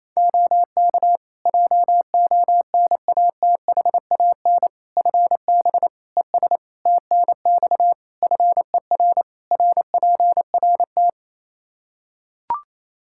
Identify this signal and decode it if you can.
Morse code, heard as a keyed tone, about 28 wpm, 700 Hz. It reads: OK JONATHAN FB ES TNX FER RPRT
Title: Practice QSO - Exchange 2 - Info - 28wpm